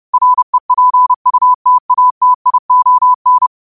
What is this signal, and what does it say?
Morse code, heard as a keyed tone, about 31 wpm, 1000 Hz. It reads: REPUTATION
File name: qotdfiles / 30wpm